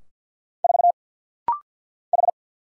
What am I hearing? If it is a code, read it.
Morse code, heard as a keyed tone, about 50 wpm, 700 Hz. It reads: 4 H
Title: Single Letter-Number Alt - Rapid-Fire 50wpm